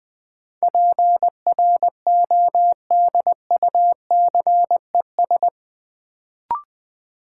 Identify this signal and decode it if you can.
Morse code, heard as a keyed tone, about 20 wpm, 700 Hz. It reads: PRODUCES
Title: Sentences from Top 1000 Words - Review Part 6 - New Word Drill - 20wpm